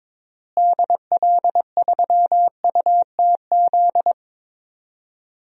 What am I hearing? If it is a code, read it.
Morse code, heard as a keyed tone, about 22 wpm, 700 Hz. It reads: DL3UTZ
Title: DXCC-100 Call Signs - 6 Characters - Repeated 3x - 22wpm